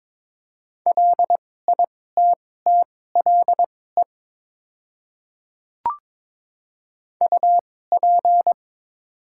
Morse code, heard as a keyed tone, about 22 wpm, spaced as for 16 wpm, 690 Hz. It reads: LITTLE UP
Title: 2x Character Spacing - Top 200 Words - 22wpm